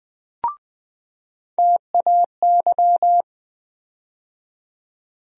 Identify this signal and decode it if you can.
Morse code, heard as a keyed tone, about 20 wpm, 700 Hz. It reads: TAY
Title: USA Call Sign Postfixes - 3 Letters 20wpm